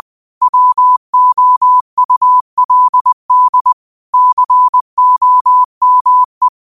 Morse code, heard as a keyed tone, about 20 wpm, 1000 Hz. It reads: WOULD COME